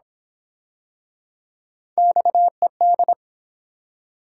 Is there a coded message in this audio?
Yes: Morse code from 2.0 to 3.1 s, heard as a keyed tone, about 29 wpm, 700 Hz. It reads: XED